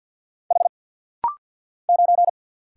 Morse code, heard as a keyed tone, about 49 wpm, 700 Hz. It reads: H 9
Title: Single Letter-Number - Rapid-Fire 50wpm